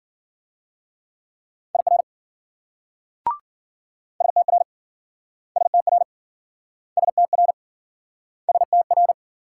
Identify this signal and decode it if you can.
Morse code, heard as a keyed tone, about 56 wpm, 700 Hz. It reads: IR STR STR STR STR